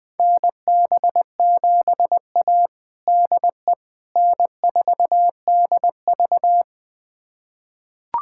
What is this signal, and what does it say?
Morse code, heard as a keyed tone, about 20 wpm, 700 Hz. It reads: NB7A DE N4DV
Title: USA Callsign DE Callsign - 4 Characters - 20wpm